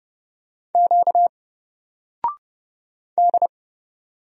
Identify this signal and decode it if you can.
Morse code, heard as a keyed tone, about 30 wpm, 700 Hz. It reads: Q D
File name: Single-Letter-Number-30wpm